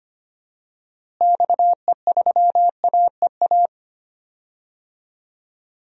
Morse code, heard as a keyed tone, about 25 wpm, 700 Hz. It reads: XE3AEA
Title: DXCC-100 Call Signs - 6 Characters - Repeated 3x - 25wpm